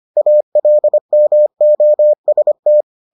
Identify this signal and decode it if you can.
Morse code, heard as a keyed tone, about 25 wpm, 600 Hz. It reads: ALMOST